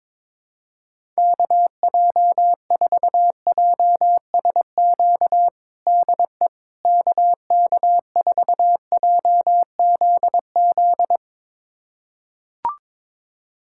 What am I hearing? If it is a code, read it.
Morse code, heard as a keyed tone, about 22 wpm, 700 Hz. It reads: KJ4JSQ DE KK4JZZ
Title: USA Callsign DE Callsign 22wpm